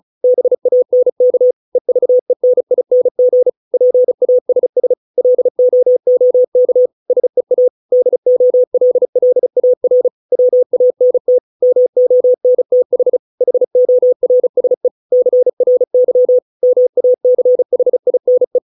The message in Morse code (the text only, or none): BANK EVENING PASS LOOK SEA DOLLAR WANT MONTH HORSE CRY MACHINE